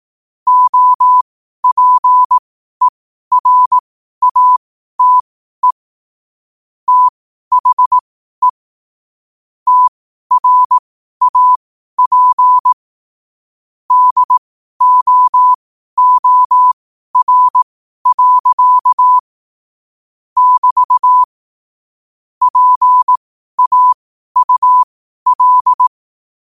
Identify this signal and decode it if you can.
Morse code, heard as a keyed tone, about 18 wpm, spaced as for 12 wpm, 1000 Hz. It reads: OPERATE THE TRAP DOOR. <BT> PAUL
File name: qotdfiles / 12wpm